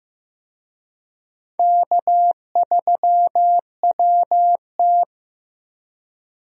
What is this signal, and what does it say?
Morse code, heard as a keyed tone, about 15 wpm, 700 Hz. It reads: K3WT
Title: USA Call Signs - 4 Characters - Repeated 3x - 15wpm